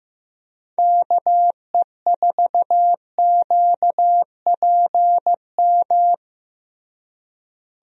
Morse code, heard as a keyed tone, about 15 wpm, 700 Hz. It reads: KE4QPM
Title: USA Call Signs - 6 Characters - Repeated 3x - 15wpm